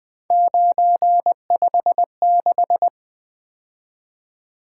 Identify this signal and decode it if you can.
Morse code, heard as a keyed tone, about 20 wpm, 700 Hz. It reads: 956
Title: Three Letter-Number Combinations - 20wpm